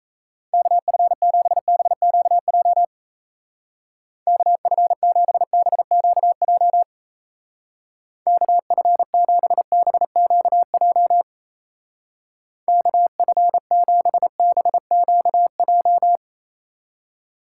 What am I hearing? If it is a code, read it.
Morse code, heard as a keyed tone, about 34 wpm, 700 Hz. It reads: KF7BQJ KF7BQJ KF7BQJ KF7BQJ
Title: Speed Racing - USA 2x3 Call Signs 28wpm